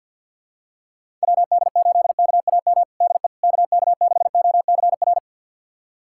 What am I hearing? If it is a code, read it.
Morse code, heard as a keyed tone, about 51 wpm, 700 Hz. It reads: WD8KAM DE KK6OCR
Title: USA Callsign DE Callsign 50wpm